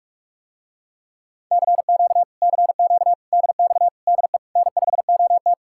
Morse code, heard as a keyed tone, about 49 wpm, 700 Hz. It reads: CQ CQ DX DE N5OT